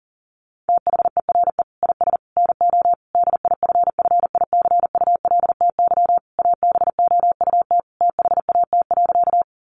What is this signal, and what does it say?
Morse code, heard as a keyed tone, about 40 wpm, 700 Hz. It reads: THERE IS NO DIFFICULTY ABOUT THAT.
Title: Sentences from Top 1500 Words 40wpm